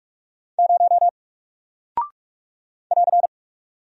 Morse code, heard as a keyed tone, about 44 wpm, 700 Hz. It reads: 0 <AR>
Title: Single Character (Letter:Number:Punctuation:Prosign) 45wpm